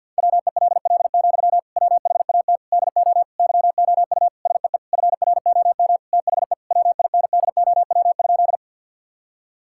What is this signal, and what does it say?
Morse code, heard as a keyed tone, about 50 wpm, 700 Hz. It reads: WELL, WHAT DO YOU SEE FROM THE WINDOW?
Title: Sentences from Top 1600 Words 50wpm